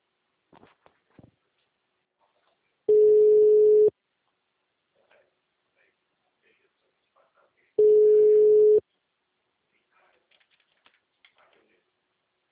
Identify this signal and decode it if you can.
Ringback tone (Europe, 425 Hz), with speech.